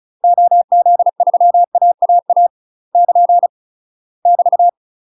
Morse code, heard as a keyed tone, about 34 wpm, 700 Hz. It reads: OZ3AAA <KN> <BT>
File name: UK MASTER QSOs TRAINING / 35WPM / UK MASTER QSOs 0025